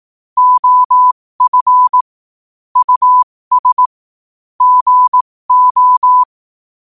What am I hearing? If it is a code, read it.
Morse code, heard as a keyed tone, about 18 wpm, spaced as for 15 wpm, 1000 Hz. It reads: OF US GO
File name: qotdfiles / 15wpm